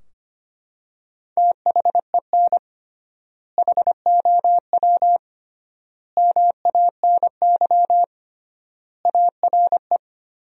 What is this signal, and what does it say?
Morse code, heard as a keyed tone, about 25 wpm, 700 Hz. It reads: THEN HOW MANY ARE
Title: 3x Word-Spacing - Sentences from Top 100 Words - 25wpm